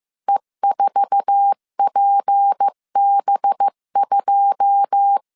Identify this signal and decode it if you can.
Morse code, heard as a keyed tone, about 15 wpm, 800 Hz. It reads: E4PB2
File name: CW / DET / 1 9 1500C S B 14wpm 3302